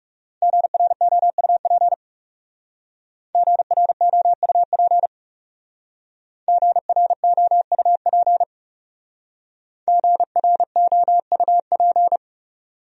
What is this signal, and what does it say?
Morse code, heard as a keyed tone, about 39 wpm, 700 Hz. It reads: GROUP GROUP GROUP GROUP